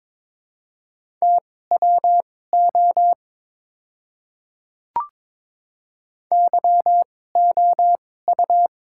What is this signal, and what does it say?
Morse code, heard as a keyed tone, about 22 wpm, spaced as for 16 wpm, 700 Hz. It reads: TWO YOU
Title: 2x Character Spacing - Top 300 Words - 22wpm